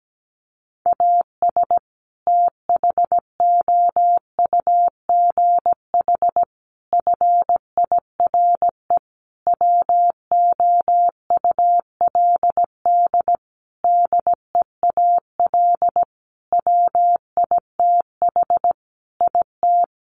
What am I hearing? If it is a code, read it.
Morse code, heard as a keyed tone, about 17 wpm, 700 Hz. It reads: AS THOUGH FIRE WOULD DEAL WITH IT
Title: Sentences from Top 1500 Words 17wpm